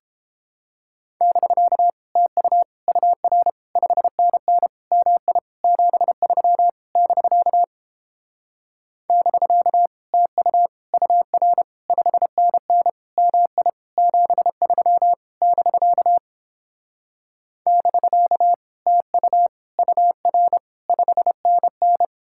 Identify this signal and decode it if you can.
Morse code, heard as a keyed tone, about 29 wpm, 700 Hz. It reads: <BK> TU UR 5NN MI 73 <BK> <BK> TU UR 5NN MI 73 <BK> <BK> TU UR 5NN